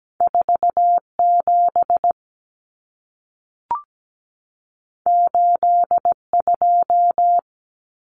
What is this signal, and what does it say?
Morse code, heard as a keyed tone, about 17 wpm, 700 Hz. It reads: 47 82
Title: Common Cut Numbers - 2-Digit 17wpm